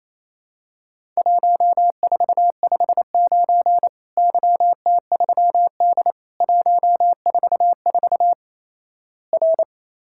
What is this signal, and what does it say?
Morse code, heard as a keyed tone, about 28 wpm, 700 Hz. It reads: 1459 YT3D 144 R
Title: Worked All Europe DX Contest - QTC Exchange - 28wpm